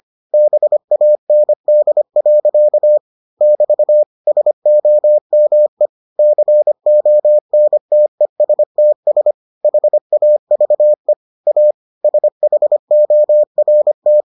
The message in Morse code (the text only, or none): BAND. <BT> SOME CONTESTS HAVE A SHORT